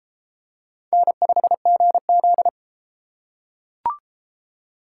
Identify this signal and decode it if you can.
Morse code, heard as a keyed tone, about 31 wpm, 700 Hz. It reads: N5GZ